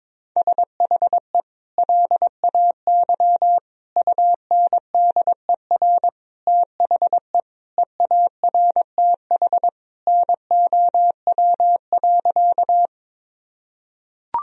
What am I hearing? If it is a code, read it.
Morse code, heard as a keyed tone, about 22 wpm, 700 Hz. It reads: SHE LAY UNDER THE EARTH NOW.
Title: Sentences from Top 1000 Words - Review Part 2 - 22wpm